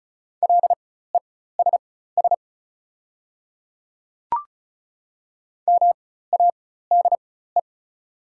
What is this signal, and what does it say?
Morse code, heard as a keyed tone, about 36 wpm, spaced as for 16 wpm, 690 Hz. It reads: LESS MADE
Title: 4x Character Spacing - Top 500 Words - 35wpm